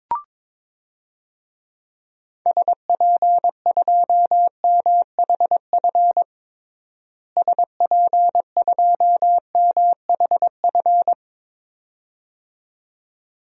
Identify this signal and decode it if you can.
Morse code, heard as a keyed tone, about 22 wpm, 700 Hz. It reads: SP2MHF SP2MHF
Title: DXCC-100 Call Signs - 6 Characters - Repeated 2x - 22wpm